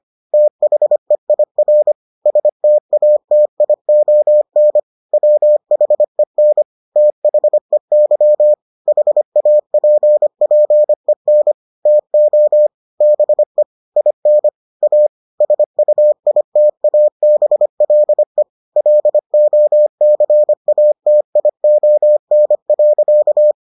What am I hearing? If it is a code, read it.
Morse code, heard as a keyed tone, about 25 wpm, 600 Hz. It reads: THEIR STATION WHEN THEY HAPPEN TO BE IN A SUITABLE LOCATION.